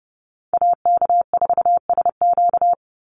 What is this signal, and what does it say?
Morse code, heard as a keyed tone, about 30 wpm, 700 Hz. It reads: AK4SQ
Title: Call Signs Repeated 2x - 30wpm